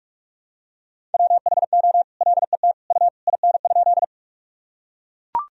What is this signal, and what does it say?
Morse code, heard as a keyed tone, about 41 wpm, 700 Hz. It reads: WHO LET U IN?
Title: Sentences from Top 300 Words and Abbreviations 45wpm